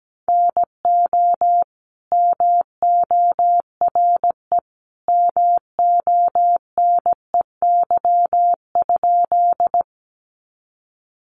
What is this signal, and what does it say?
Morse code, heard as a keyed tone, about 17 wpm, 700 Hz. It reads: NO MORE MONEY?
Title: Sentences from Top 400 Words 17wpm